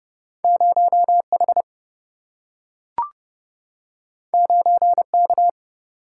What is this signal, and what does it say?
Morse code, heard as a keyed tone, about 30 wpm, 700 Hz. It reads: 0H 9K
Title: Single-Letter Single-Number 30wpm